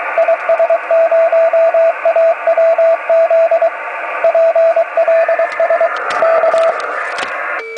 Morse code, heard as a keyed tone, about 23 wpm, 660 Hz, with radio noise. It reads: IS0AWZ PLS K
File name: om3pv 70mhz / zaznam / is0awz